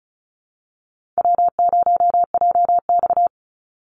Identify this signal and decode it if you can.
Morse code, heard as a keyed tone, about 34 wpm, 700 Hz. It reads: W0JX
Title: Call Signs Repeated 2x - 35wpm